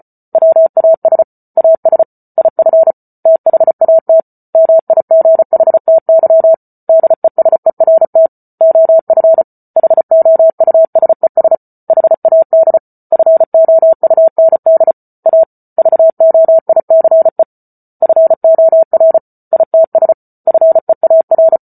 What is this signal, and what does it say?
Morse code, heard as a keyed tone, about 34 wpm, 650 Hz. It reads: WAS AS IF THAT MIGHTY DESERT OF HOUSES HAD FOUND A VOICE FOR ITS FEAR